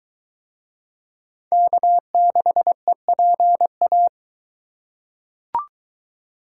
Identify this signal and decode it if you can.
Morse code, heard as a keyed tone, about 20 wpm, 700 Hz. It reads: K6EPA